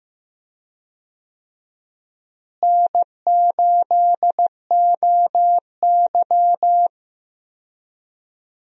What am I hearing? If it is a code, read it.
Morse code, heard as a keyed tone, about 15 wpm, 700 Hz. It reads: N8OY
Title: USA Call Signs - 4 Characters - Repeated 2x - 15wpm